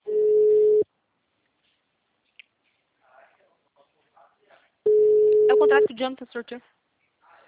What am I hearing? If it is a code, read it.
Ringback tone (Europe, 425 Hz), with speech.